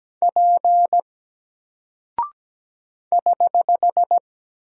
Morse code, heard as a keyed tone, about 17 wpm, 700 Hz. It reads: P <HH>
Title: Single Character (Letter:Number:Punctuation:Prosign) 17wpm